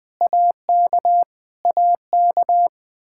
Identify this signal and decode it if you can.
Morse code, heard as a keyed tone, about 20 wpm, 700 Hz. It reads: AK AK
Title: Practice QSO - Exchange 1 - Beginning - 20wpm